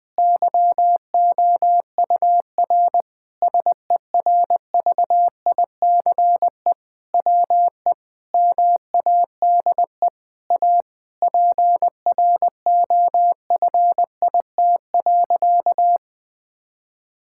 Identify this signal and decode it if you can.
Morse code, heard as a keyed tone, about 21 wpm, 700 Hz. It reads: YOUR SERVICE WE MADE A PROFIT.
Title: Sentences from Top 1000 Words - Review Part 5 - 20wpm